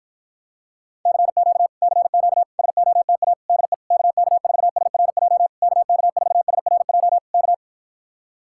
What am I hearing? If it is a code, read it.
Morse code, heard as a keyed tone, about 53 wpm, 700 Hz. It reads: CQ CQ SOTA DE KK4SRJ KK4SRJ K